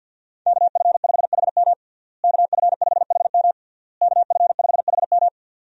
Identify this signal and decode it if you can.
Morse code, heard as a keyed tone, about 52 wpm, 700 Hz. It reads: KF5HM KF5HM KF5HM
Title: USA Call Signs - 5 Characters - Repeated 3x - 50wpm